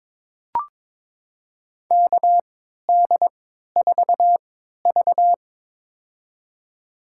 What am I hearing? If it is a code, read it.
Morse code, heard as a keyed tone, about 22 wpm, spaced as for 13 wpm, 700 Hz. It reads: KD4V
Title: USA Call Signs - 4 Characters - 3x Character Spacing - 22wpm